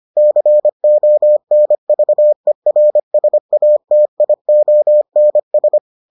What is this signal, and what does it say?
Morse code, heard as a keyed tone, about 25 wpm, 600 Hz. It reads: CONVERSATIONS